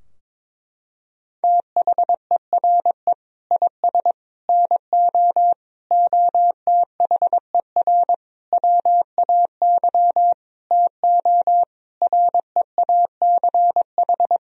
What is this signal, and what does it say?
Morse code, heard as a keyed tone, about 22 wpm, 700 Hz. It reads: THERE IS NO OTHER WAY TO REACH
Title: Sentences from Top 1000 Words - Review Part 2 - 22wpm